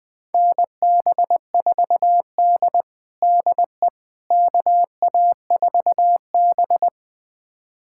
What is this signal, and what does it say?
Morse code, heard as a keyed tone, about 20 wpm, 700 Hz. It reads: NB4D DE KA4B
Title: USA Callsign DE Callsign - 4 Characters - 20wpm